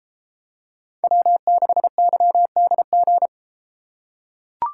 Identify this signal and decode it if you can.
Morse code, heard as a keyed tone, about 32 wpm, 700 Hz. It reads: W6YDG